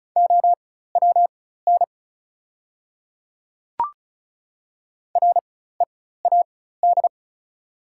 Morse code, heard as a keyed tone, about 34 wpm, spaced as for 16 wpm, 700 Hz. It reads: OWN READ
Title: 4x Character Spacing - Top 300 Words - 35wpm